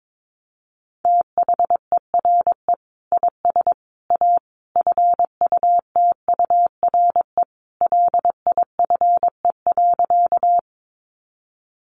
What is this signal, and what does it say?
Morse code, heard as a keyed tone, about 22 wpm, 700 Hz. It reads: THERE IS A FUTURE LIFE.
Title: Sentences from Top 700 Words 22wpm